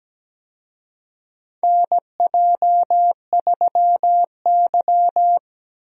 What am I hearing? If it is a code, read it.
Morse code, heard as a keyed tone, about 17 wpm, 700 Hz. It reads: NJ3Y
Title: USA Call Signs - 4 Characters - Repeated 3x - 17wpm